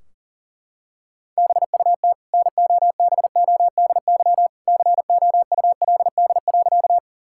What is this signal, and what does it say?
Morse code, heard as a keyed tone, about 40 wpm, 700 Hz. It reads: BUT NOBODY COULD.
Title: Sentences from Top 1600 Words 40wpm